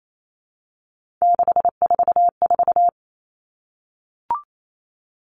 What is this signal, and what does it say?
Morse code, heard as a keyed tone, about 28 wpm, 700 Hz. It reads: 644
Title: All Cut Numbers - 3-Digit 28wpm